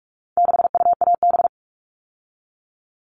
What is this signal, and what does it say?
Morse code, heard as a keyed tone, about 44 wpm, 700 Hz. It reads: 6UAB
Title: All Cut Numbers - 4-Digit 45wpm